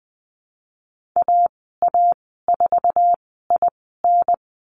Morse code, heard as a keyed tone, about 20 wpm, spaced as for 15 wpm, 700 Hz. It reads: AA4IN
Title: 048 Lesson 9g - Number 4 - Call Signs 20wpm